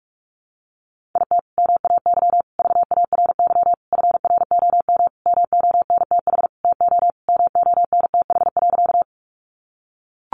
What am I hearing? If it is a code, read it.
Morse code, heard as a keyed tone, about 44 wpm, 700 Hz. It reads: IT MAY VARY FROM MONTH TO MONTH.
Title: Sentences from Top 1200 Words 45wpm